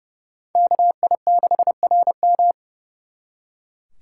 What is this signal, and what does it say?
Morse code, heard as a keyed tone, about 37 wpm, 700 Hz. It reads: KI6RM
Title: Speed Racing - USA 2x2 Call Signs 30wpm